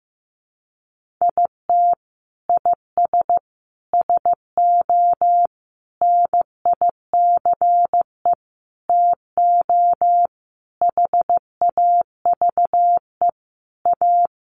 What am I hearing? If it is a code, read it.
Morse code, heard as a keyed tone, about 15 wpm, 700 Hz. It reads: IT IS SO NICE TO HAVE A
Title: Sentences from Top 1100 Words 15wpm